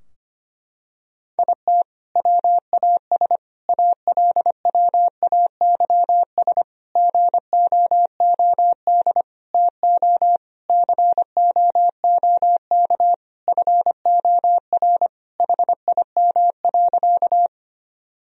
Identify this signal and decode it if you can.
Morse code, heard as a keyed tone, about 25 wpm, 700 Hz. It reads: IT WAS ALWAYS GOOD TO COOK FOR HIM.
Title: Sentences from Top 1000 Words - Review Part 4 - 25wpm